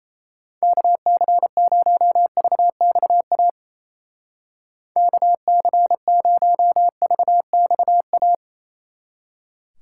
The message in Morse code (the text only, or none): KC0VXA KC0VXA